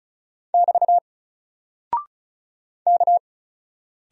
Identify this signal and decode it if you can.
Morse code, heard as a keyed tone, about 36 wpm, 700 Hz. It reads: <BT> K
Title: Single Character (Letter:Number:Punctuation:Prosign) Alt 35wpm